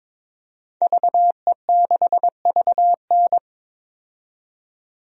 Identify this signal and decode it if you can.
Morse code, heard as a keyed tone, about 21 wpm, 700 Hz. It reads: VE6VN
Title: DXCC-100 Call Signs - 5 Characters - 22wpm